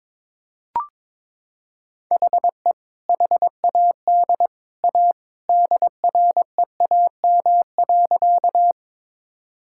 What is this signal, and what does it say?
Morse code, heard as a keyed tone, about 22 wpm, 700 Hz. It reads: HE HAD A DREAM.
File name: practice / 22wpm / Sentences with 2-Word Phrases - 22wpm